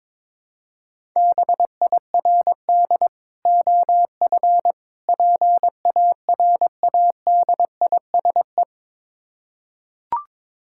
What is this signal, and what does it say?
Morse code, heard as a keyed tone, about 22 wpm, 690 Hz. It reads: BIRD OF PARADISE
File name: Difficult Words with Hints - 22wpm